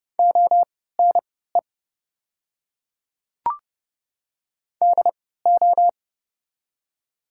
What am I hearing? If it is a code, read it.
Morse code, heard as a keyed tone, about 32 wpm, spaced as for 17 wpm, 690 Hz. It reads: ONE DO
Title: 3x Character Spacing - Top 200 Words - 30wpm